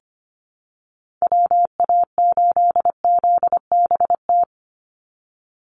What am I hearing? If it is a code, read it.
Morse code, heard as a keyed tone, about 25 wpm, 700 Hz. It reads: WA8ZBT
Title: Call Signs Repeated 3x - 25wpm